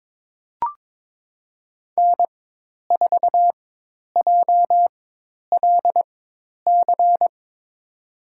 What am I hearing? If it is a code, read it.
Morse code, heard as a keyed tone, about 22 wpm, spaced as for 10 wpm, 700 Hz. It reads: N4JLC
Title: USA Call Signs - 5 Characters - 4x Character Spacing - 22wpm